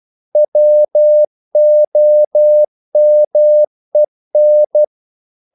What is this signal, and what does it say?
Morse code, heard as a keyed tone, about 12 wpm, 600 Hz. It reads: WOMEN
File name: CW / Practice / EnglishQuotations0312